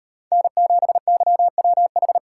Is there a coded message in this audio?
Yes: Morse code from 0.3 to 2.2 s, heard as a keyed tone, about 38 wpm, 700 Hz. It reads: N7YWH